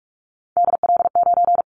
Morse code, heard as a keyed tone, about 46 wpm, 700 Hz. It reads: DL9
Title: Callsign Prefixes for Top 25 DXCC - 45wpm